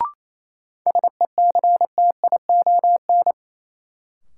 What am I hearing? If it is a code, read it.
Morse code, heard as a keyed tone, about 28 wpm, 700 Hz. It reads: SECTION
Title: Sentences from Top 1000 Words - Review Part 5 - New Word Drill - 28wpm